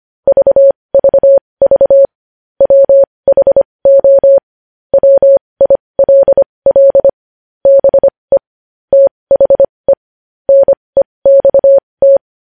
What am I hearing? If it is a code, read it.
Morse code, heard as a keyed tone, about 25 wpm, spaced as for 20 wpm, 550 Hz. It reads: VVV WHO WILL BE THE NEXT
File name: podcast / News headlines-1x-25x20